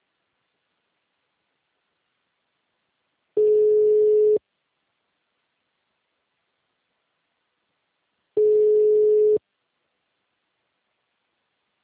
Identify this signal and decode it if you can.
Ringback tone (Europe, 425 Hz), with speech.